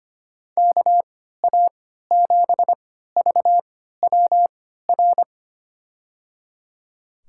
Morse code, heard as a keyed tone, about 25 wpm, spaced as for 14 wpm, 700 Hz. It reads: KA7VWR
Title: USA Call Signs - 3x Character Spacing - 25wpm